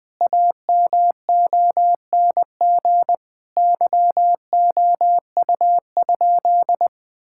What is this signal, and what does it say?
Morse code, heard as a keyed tone, about 21 wpm, 700 Hz. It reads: AMONG YOU?
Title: Sentences from Top 1600 Words 20wpm